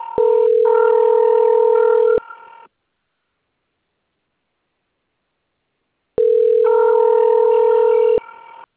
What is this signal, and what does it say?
Ringback tone (North America), 440 + 480 Hz.